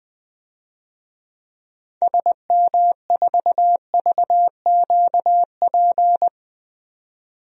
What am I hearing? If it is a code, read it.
Morse code, heard as a keyed tone, about 20 wpm, 700 Hz. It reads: SM4VQP
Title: DXCC-100 Call Signs - 6 Characters - 20wpm